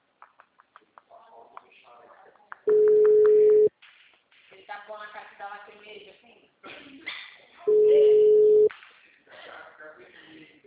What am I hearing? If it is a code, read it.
Ringback tone (Europe, 425 Hz), with speech.
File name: out- -2051- - - .8646